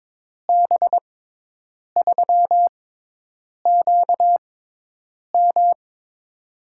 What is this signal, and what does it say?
Morse code, heard as a keyed tone, about 22 wpm, spaced as for 8 wpm, 700 Hz. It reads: B3QM
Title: USA Call Signs - 5 Characters - 6x Character Spacing - 22wpm